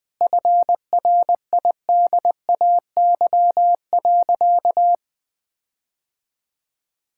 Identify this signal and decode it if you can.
Morse code, heard as a keyed tone, about 21 wpm, 700 Hz. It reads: FRIDAY.
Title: Sentences with 7-Word Phrases - 20wpm